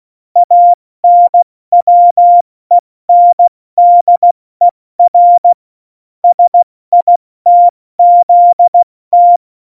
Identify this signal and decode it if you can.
Morse code, heard as a keyed tone, about 16 wpm, 700 Hz. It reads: ANWENDER SITZT
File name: Morsekurs / Witze / Witz 22. Januar 2019, 70 F80